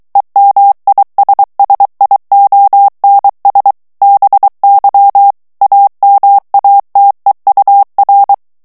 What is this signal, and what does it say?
Morse code, heard as a keyed tone, about 23 wpm, 800 Hz. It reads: EMISSIONS BY AMATEUR
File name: ANGLAIS texte clair 3 vitesse 22 mots minute